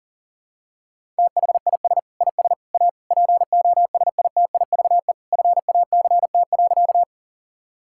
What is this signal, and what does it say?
Morse code, heard as a keyed tone, about 40 wpm, 700 Hz. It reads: THIS IS A POSITIVE FACT.
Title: Sentences from Top 1000 Words - Review Part 5 - 40wpm